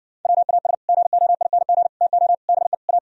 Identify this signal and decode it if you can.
Morse code, heard as a keyed tone, about 60 wpm, 700 Hz. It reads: WAS GOING TO BE A